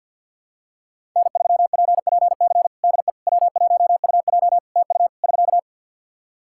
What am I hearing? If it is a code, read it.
Morse code, heard as a keyed tone, about 51 wpm, 700 Hz. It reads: N3PPC DE W1UJ TU <SK>
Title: Practice QSO - Exchange 4 - Ending - 50wpm